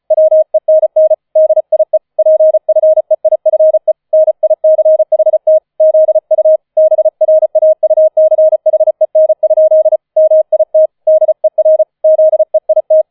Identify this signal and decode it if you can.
Morse code, heard as a keyed tone, about 36 wpm, 630 Hz. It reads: WENN DIE PFEIFE NICHT ZU BRAUCHEN? MIT DER ZEIT
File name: 004 Vierter Streich 35wpm